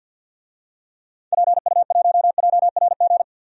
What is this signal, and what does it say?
Morse code, heard as a keyed tone, about 53 wpm, 700 Hz. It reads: PU1JRG
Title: DXCC-100 Call Signs - 6 Characters - Repeated 2x - 50wpm